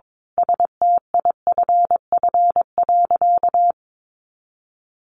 Morse code, heard as a keyed tone, about 22 wpm, 700 Hz. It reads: STIFF.
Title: A Sound Of Thunder 22wpm